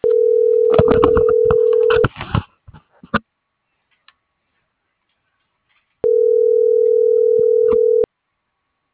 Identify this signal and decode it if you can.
Ringback tone (North America), 440 + 480 Hz.